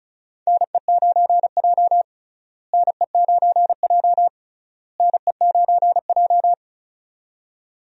Morse code, heard as a keyed tone, about 36 wpm, 700 Hz. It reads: NE9J NE9J NE9J
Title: USA Call Signs - 4 Characters - Repeated 3x - 35wpm